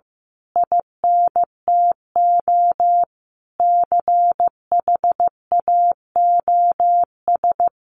Morse code, heard as a keyed tone, about 15 wpm, 700 Hz. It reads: INTO CHAOS
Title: A Sound Of Thunder 15wpm